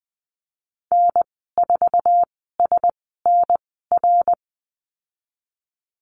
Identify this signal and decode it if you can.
Morse code, heard as a keyed tone, about 20 wpm, spaced as for 15 wpm, 700 Hz. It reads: N4SNR
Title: 055 Lesson 10g - Letter r - Call Signs 20wpm